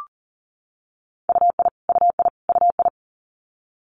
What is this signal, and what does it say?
Morse code, heard as a keyed tone, about 40 wpm, 700 Hz. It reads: UI UI UI
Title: Multi-letter Phonemes Repeated 3x 40wpm